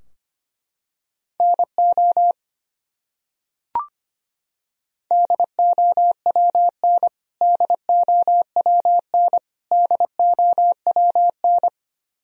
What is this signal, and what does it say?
Morse code, heard as a keyed tone, about 25 wpm, 700 Hz. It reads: NO DOWN DOWN DOWN
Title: Top 100 Words Repeated 25wpm